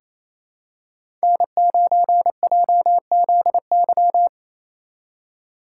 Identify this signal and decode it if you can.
Morse code, heard as a keyed tone, about 28 wpm, 700 Hz. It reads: N9JZY
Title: K1USN SST QSO Practice - 28wpm